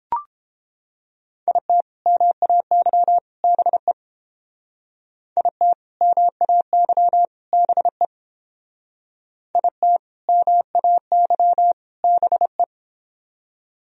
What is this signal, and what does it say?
Morse code, heard as a keyed tone, about 30 wpm, 700 Hz. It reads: IT MAY BE IT MAY BE IT MAY BE